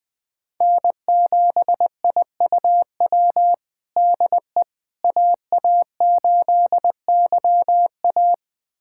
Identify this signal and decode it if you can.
Morse code, heard as a keyed tone, about 21 wpm, 700 Hz. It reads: N7IUW DE AA8YA
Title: USA Callsign DE Callsign - 5 Characters - 20wpm